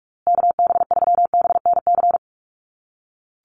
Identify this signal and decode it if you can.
Morse code, heard as a keyed tone, about 44 wpm, 700 Hz. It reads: KB3BNC
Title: USA Call Signs Repeated 3x 45wpm